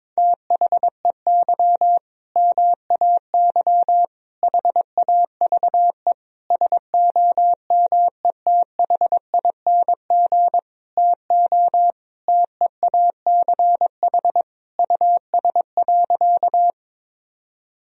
Morse code, heard as a keyed tone, about 22 wpm, 700 Hz. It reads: THEY MAY HAVE SOMETHING TO TEACH US.
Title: Sentences from Top 1000 Words - Review Part 2 - 22wpm